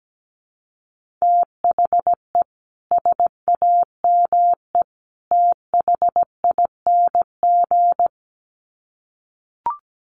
Morse code, heard as a keyed tone, about 17 wpm, 700 Hz. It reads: THE SAME THING
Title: Sets of 3 Words 17wpm